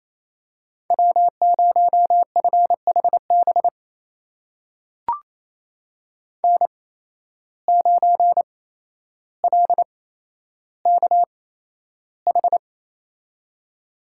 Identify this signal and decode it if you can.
Morse code, heard as a keyed tone, about 28 wpm, spaced as for 12 wpm, 700 Hz. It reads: W0FHB N9LKH
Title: USA Call Signs - 5 Characters - 8x Character Spacing - 28wpm